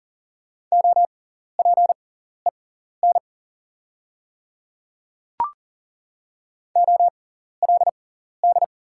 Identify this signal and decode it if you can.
Morse code, heard as a keyed tone, about 39 wpm, spaced as for 14 wpm, 690 Hz. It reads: OPEN OLD
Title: 6x Character Spacing - Top 500 Words - 40wpm